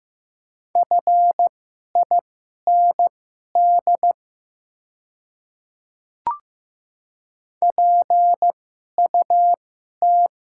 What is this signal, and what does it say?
Morse code, heard as a keyed tone, about 15 wpm, spaced as for 11 wpm, 700 Hz. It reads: FIND PUT
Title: 2x Character Spacing - Top 500 Words - 15wpm